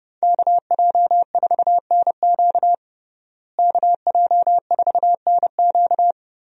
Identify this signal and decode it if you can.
Morse code, heard as a keyed tone, about 30 wpm, 700 Hz. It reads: KJ4NQ KJ4NQ
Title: USA Call Signs - 5 Characters - Repeated 3x - 30wpm